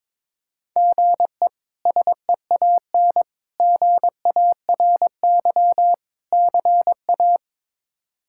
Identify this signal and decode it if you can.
Morse code, heard as a keyed tone, about 22 wpm, 700 Hz. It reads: GE SEAN GARY CA
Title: K1USN SST QSO Practice - 22wpm